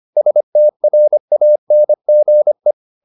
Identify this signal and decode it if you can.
Morse code, heard as a keyed tone, about 25 wpm, 600 Hz. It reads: STRANGE
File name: the old man and the sea part 1 25wpm